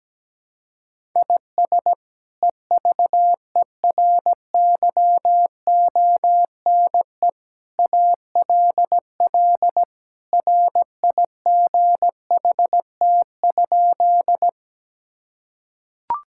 Morse code, heard as a keyed tone, about 17 wpm, 700 Hz. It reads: IS EVERYONE ALL RIGHT?
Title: Sentences from Top 1000 Words - Review Part 5 - 17wpm